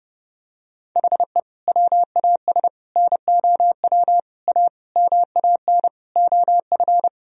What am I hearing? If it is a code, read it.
Morse code, heard as a keyed tone, about 30 wpm, 700 Hz. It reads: HE WAS NOW A MAN OF
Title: Sentences from Top 1000 Words - Review Part 3 - 30wpm